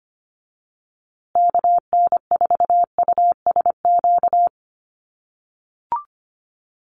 Morse code, heard as a keyed tone, about 25 wpm, 700 Hz. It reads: KN4USQ
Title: USA Call Signs Repeated 3x 25wpm